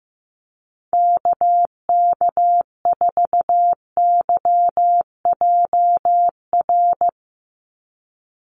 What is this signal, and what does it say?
Morse code, heard as a keyed tone, about 15 wpm, 700 Hz. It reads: KK4YJR
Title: USA Call Signs 15wpm